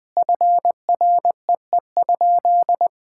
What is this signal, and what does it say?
Morse code, heard as a keyed tone, about 20 wpm, 700 Hz. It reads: FREE?
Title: Sentences with 6-Word Phrases - Spoken Beginning - 20wpm